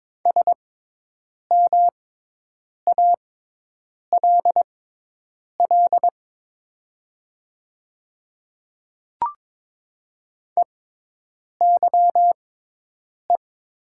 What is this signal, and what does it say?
Morse code, heard as a keyed tone, about 22 wpm, spaced as for 8 wpm, 700 Hz. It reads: SMALL EYE
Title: 6x Character Spacing - Top 500 Words - 22wpm